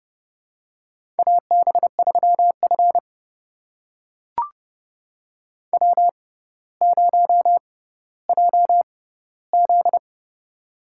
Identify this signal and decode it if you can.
Morse code, heard as a keyed tone, about 30 wpm, spaced as for 15 wpm, 700 Hz. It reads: AB3F W0JZ
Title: USA Call Signs - 4 Characters - 6x Character Spacing - 30wpm